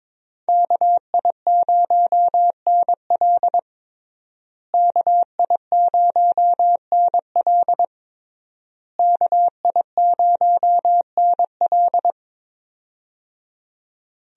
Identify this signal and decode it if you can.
Morse code, heard as a keyed tone, about 22 wpm, 700 Hz. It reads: KI0NL KI0NL KI0NL
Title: USA Call Signs - 5 Characters - Repeated 3x - 22wpm